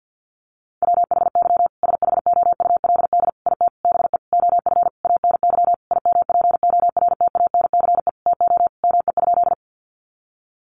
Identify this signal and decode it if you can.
Morse code, heard as a keyed tone, about 50 wpm, 700 Hz. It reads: WHY SHOULD IT BE OF ANY IMPORTANCE TO ME?
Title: Sentences from Top 1200 Words 50wpm